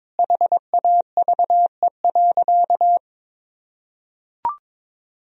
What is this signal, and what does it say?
Morse code, heard as a keyed tone, about 22 wpm, 700 Hz. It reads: HAVE.
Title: Sentences from Top 1000 Words - Review Part 3 - 22wpm